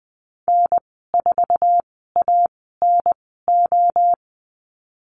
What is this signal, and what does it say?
Morse code, heard as a keyed tone, about 20 wpm, spaced as for 15 wpm, 700 Hz. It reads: N4ANO
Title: 055 Lesson 10g - Letter r - Call Signs 20wpm